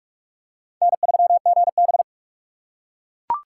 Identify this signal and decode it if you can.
Morse code, heard as a keyed tone, about 44 wpm, 700 Hz. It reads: N3GB
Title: USA Call Signs - 4 Characters - 3x Character Spacing - 45wpm